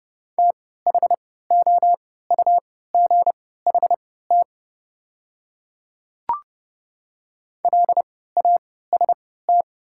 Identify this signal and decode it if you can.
Morse code, heard as a keyed tone, about 30 wpm, spaced as for 17 wpm, 700 Hz. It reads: THOUGHT LAST
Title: 3x Character Spacing - Top 300 Words - 30wpm